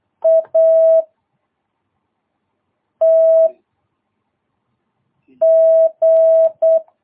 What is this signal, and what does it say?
Morse code, heard as a keyed tone, about 8 wpm, 650 Hz. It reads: A T G
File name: jh8cbh / 2022cw